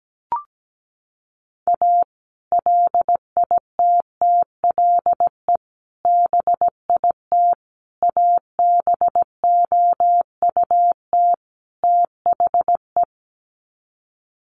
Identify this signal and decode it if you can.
Morse code, heard as a keyed tone, about 17 wpm, 700 Hz. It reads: A LITTLE BIT ABOUT THE
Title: Sets of 5 Words Encore2 17wpm